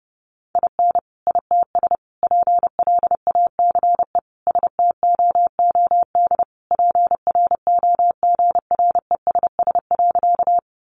Morse code, heard as a keyed tone, about 30 wpm, 700 Hz. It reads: IN ITS PLACE STOOD PROGRESS.
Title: Sentences from Top 1400 Words 30wpm